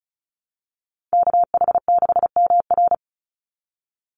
Morse code, heard as a keyed tone, about 34 wpm, 700 Hz. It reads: KH6MR
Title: USA Call Signs 35wpm